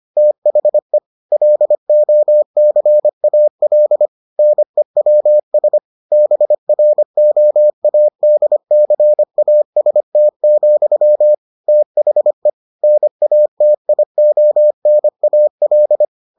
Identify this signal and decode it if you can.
Morse code, heard as a keyed tone, about 25 wpm, 600 Hz. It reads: THE LOCAL NEWS BROADCAST, THE NATIONAL